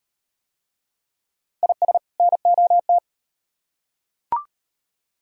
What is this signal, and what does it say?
Morse code, heard as a keyed tone, about 38 wpm, 700 Hz. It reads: IS NOT